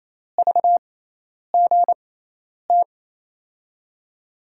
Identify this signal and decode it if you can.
Morse code, heard as a keyed tone, about 28 wpm, spaced as for 10 wpm, 700 Hz. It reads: VGT
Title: USA Call Signs - 6 Characters - 6x Character Spacing - 28wpm